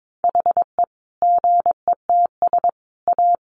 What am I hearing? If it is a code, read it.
Morse code, heard as a keyed tone, about 22 wpm, 700 Hz. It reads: HE GETS A
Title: Additional Easy Phrases 22wpm